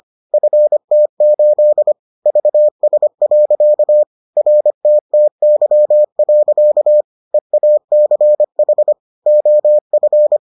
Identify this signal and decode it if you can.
Morse code, heard as a keyed tone, about 25 wpm, 600 Hz. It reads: FT8 VS. RTTY. EACH OF